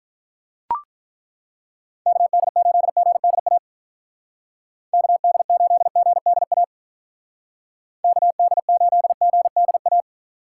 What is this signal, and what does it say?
Morse code, heard as a keyed tone, about 43 wpm, 700 Hz. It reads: KD8GDA KD8GDA KD8GDA
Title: Speed Racing - USA Call Signs 35wpm